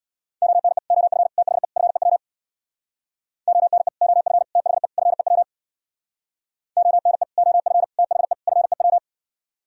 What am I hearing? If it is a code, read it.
Morse code, heard as a keyed tone, about 64 wpm, 700 Hz. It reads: ONE OF THE FEW ONE OF THE FEW ONE OF THE FEW